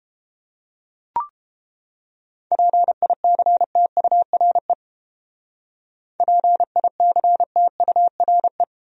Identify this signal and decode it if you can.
Morse code, heard as a keyed tone, about 32 wpm, 700 Hz. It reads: PICTURE PICTURE